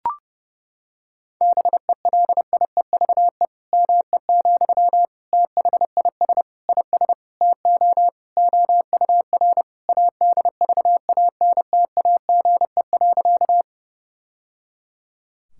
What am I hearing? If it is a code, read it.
Morse code, heard as a keyed tone, about 30 wpm, 700 Hz. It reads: BELIEVE ME, THIS IS TO OUR ADVANTAGE.
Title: Sentences from Top 1000 Words - Review Part 6 - 30wpm